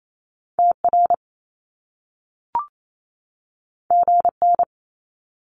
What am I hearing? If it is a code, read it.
Morse code, heard as a keyed tone, about 28 wpm, 700 Hz. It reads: TR GN
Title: Top Two Letter Combinations in Words 28wpm